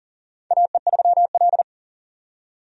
Morse code, heard as a keyed tone, about 42 wpm, 690 Hz. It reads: AE3L
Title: USA Call Signs - 4 Characters - 40wpm